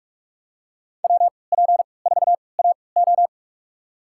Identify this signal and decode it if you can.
Morse code, heard as a keyed tone, about 45 wpm, spaced as for 25 wpm, 700 Hz. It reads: WP4AO
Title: USA Call Signs - 5 Characters - 3x Character Spacing - 45wpm